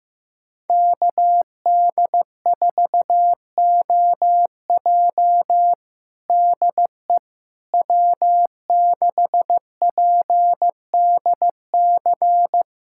Morse code, heard as a keyed tone, about 15 wpm, 700 Hz. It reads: KD4OJ DE W6PDC
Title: USA Callsign DE Callsign - 5 Characters - 15wpm